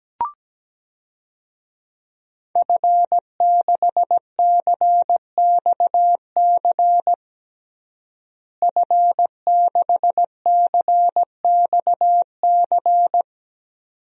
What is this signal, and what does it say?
Morse code, heard as a keyed tone, about 17 wpm, 700 Hz. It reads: F6CXC F6CXC
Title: DXCC-100 Call Signs - 5 Characters - Repeated 2x - 17wpm